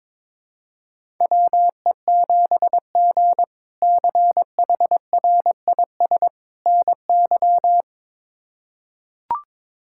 Morse code, heard as a keyed tone, about 22 wpm, 700 Hz. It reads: WE7G CHRIS NY
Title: K1USN SST QSO Practice - 22wpm